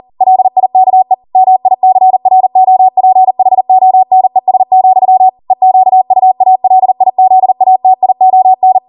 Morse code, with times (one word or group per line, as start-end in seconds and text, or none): 0.2-1.1 LIKE
1.4-5.3 MICROPHONES,
5.5-8.8 EQUALIZATION